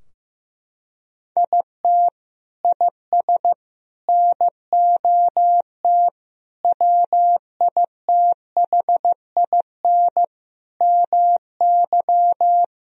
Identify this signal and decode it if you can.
Morse code, heard as a keyed tone, about 15 wpm, 700 Hz. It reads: IT IS NOT WITHIN MY
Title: Sentences from Top 1000 Words - Review Part 4 - 15wpm